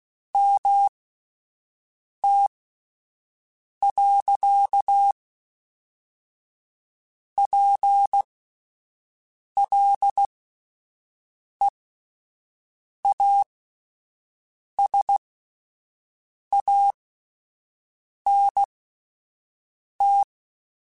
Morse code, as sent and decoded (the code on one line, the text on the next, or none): -- - .-.-.- .--. .-.. .- ... .- -. -
MT.PLEASANT